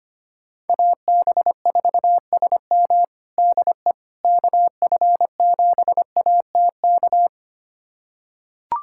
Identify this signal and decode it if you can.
Morse code, heard as a keyed tone, about 25 wpm, 700 Hz. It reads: AB4SM DE KF7ATK
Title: USA Callsign DE Callsign 25wpm